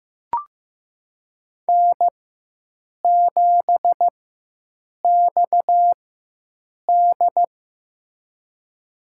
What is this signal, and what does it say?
Morse code, heard as a keyed tone, about 15 wpm, spaced as for 7 wpm, 700 Hz. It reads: N7XD
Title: USA Call Signs - 4 Characters - 4x Character Spacing - 15wpm